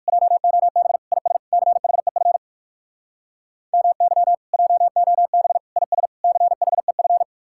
Morse code, heard as a keyed tone, about 59 wpm, 700 Hz. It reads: JOB IS CHEF MY JOB IS CHEF